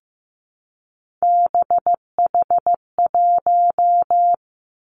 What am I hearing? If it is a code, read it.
Morse code, heard as a keyed tone, about 15 wpm, 700 Hz. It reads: BH1
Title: Three Challenging Letter-Number Combinations 15wpm